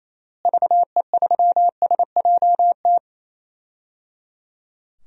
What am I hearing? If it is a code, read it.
Morse code, heard as a keyed tone, about 27 wpm, 700 Hz. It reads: VE3SJT
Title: DXCC-100 Call Signs - 6 Characters - Repeated 3x - 28wpm